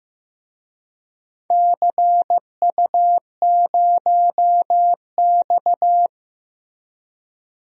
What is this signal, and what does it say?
Morse code, heard as a keyed tone, about 15 wpm, 700 Hz. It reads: CU0X
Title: DXCC-100 Call Signs - 4 Characters - Repeated 2x - 15wpm